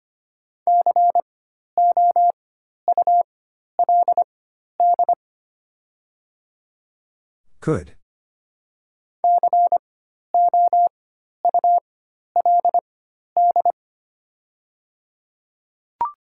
Morse code, heard as a keyed tone, about 25 wpm, spaced as for 12 wpm, 700 Hz. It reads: COULD COULD
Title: 4x Character Spacing - Top 100 Words - 25wpm